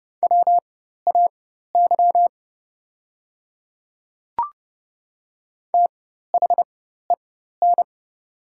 Morse code, heard as a keyed tone, about 30 wpm, spaced as for 14 wpm, 700 Hz. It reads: WAY THEN
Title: 4x Character Spacing - Top 100 Words - 30wpm